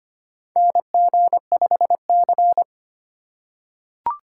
Morse code, heard as a keyed tone, about 25 wpm, 700 Hz. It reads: NG5C
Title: USA Call Signs - 4 Characters - 2x Character Spacing - 25wpm